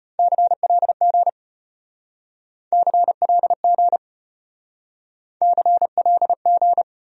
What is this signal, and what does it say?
Morse code, heard as a keyed tone, about 33 wpm, 700 Hz. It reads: CLG CLG CLG